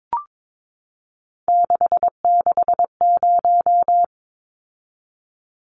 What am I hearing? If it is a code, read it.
Morse code, heard as a keyed tone, about 22 wpm, 700 Hz. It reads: 660
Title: Numbers - 3-Digit 22wpm